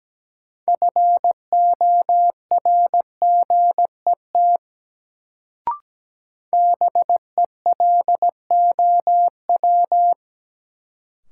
Morse code, heard as a keyed tone, about 17 wpm, 700 Hz. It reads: FORGET BELOW
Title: Sentences from Top 1200 Words - New Word Drill Alt - 17wpm